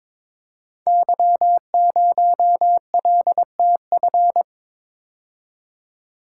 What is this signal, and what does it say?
Morse code, heard as a keyed tone, about 22 wpm, 700 Hz. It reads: Y0LTF
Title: Code Groups - 5 Letter:Numbers - 22wpm